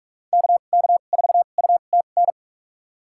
Morse code, heard as a keyed tone, about 46 wpm, spaced as for 32 wpm, 700 Hz. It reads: KK4UTN
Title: USA Call Signs - 2x Character Spacing - 45wpm